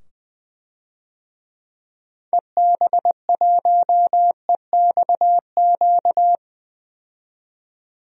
Morse code, heard as a keyed tone, about 21 wpm, 700 Hz. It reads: EB1EXQ
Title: DXCC-100 Call Signs - 6 Characters - 20wpm